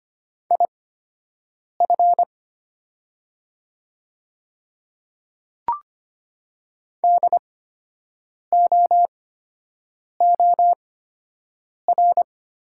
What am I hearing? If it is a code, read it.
Morse code, heard as a keyed tone, about 25 wpm, spaced as for 7 wpm, 700 Hz. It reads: IF DOOR
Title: 8x Character Spacing - Top 500 Words - 25wpm